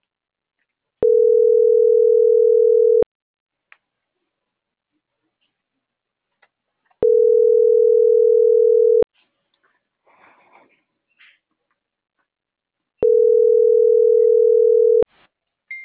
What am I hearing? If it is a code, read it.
Ringback tone (North America), 440 + 480 Hz.